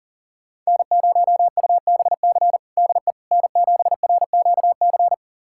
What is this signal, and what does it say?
Morse code, heard as a keyed tone, about 40 wpm, 700 Hz. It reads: N0UBC DE N7RQC
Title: USA Callsign DE Callsign - 5 Characters - 40wpm